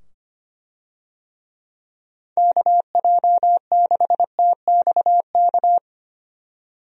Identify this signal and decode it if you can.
Morse code, heard as a keyed tone, about 25 wpm, 700 Hz. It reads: KJ6TXK
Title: USA Call Signs - 6 Characters - 25wpm